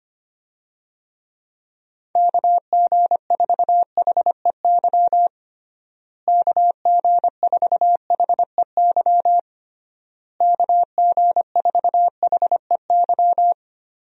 Morse code, heard as a keyed tone, about 25 wpm, 700 Hz. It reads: KG4HEY KG4HEY KG4HEY
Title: USA Call Signs - 6 Characters - Repeated 3x - 25wpm